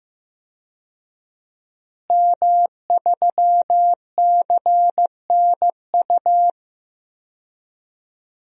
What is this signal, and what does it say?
Morse code, heard as a keyed tone, about 15 wpm, 700 Hz. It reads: M3CNU
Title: DXCC-100 Call Signs - 5 Characters - Repeated 3x - 15wpm